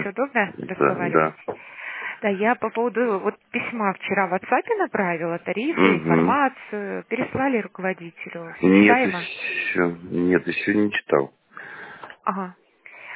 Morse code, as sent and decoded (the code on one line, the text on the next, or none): none